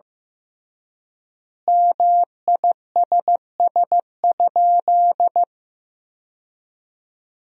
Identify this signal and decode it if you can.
Morse code, heard as a keyed tone, about 15 wpm, 700 Hz. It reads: MISS?
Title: 3x Word-Spacing - Sentences from Top 500 Words - 15wpm